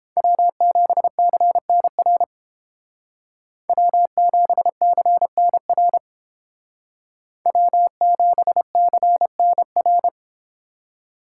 Morse code, heard as a keyed tone, about 29 wpm, 700 Hz. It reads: W7CNR W7CNR W7CNR